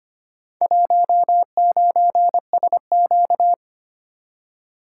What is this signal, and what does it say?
Morse code, heard as a keyed tone, about 25 wpm, 700 Hz. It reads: 19SQ
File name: practice / 25wpm / Four Letter-Number Combinations - 25wpm